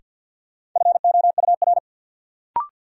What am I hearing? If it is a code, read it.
Morse code, heard as a keyed tone, about 53 wpm, 700 Hz. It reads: FOUR
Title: Sentences from Top 300 Words - New Word Drill - 50wpm